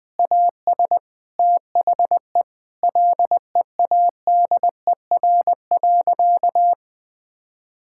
Morse code, heard as a keyed tone, about 20 wpm, 700 Hz. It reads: AS THE LEADER.
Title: Sentences with 3-Word Phrases - Spoken Beginning - 20wpm